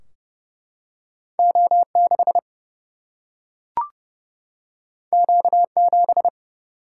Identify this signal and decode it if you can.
Morse code, heard as a keyed tone, about 30 wpm, 700 Hz. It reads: O 6 Q 7
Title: Single-Letter Single-Number 30wpm